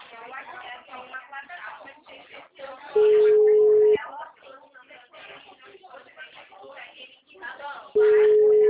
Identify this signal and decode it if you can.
Ringback tone (Europe, 425 Hz), with speech.